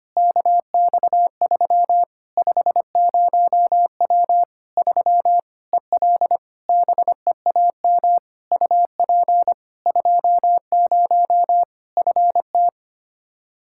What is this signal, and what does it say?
Morse code, heard as a keyed tone, about 25 wpm, 700 Hz. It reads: KX3 50W 3 EL BEAM UP 20 FT
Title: Practice QSO - Exchange 2 - Info - 25wpm